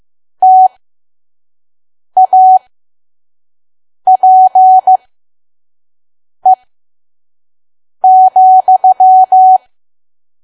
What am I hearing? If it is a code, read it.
Morse code, heard as a keyed tone, about 15 wpm, spaced as for 5 wpm, 750 Hz. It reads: TAPE,
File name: Archive / 5 WPM files / PM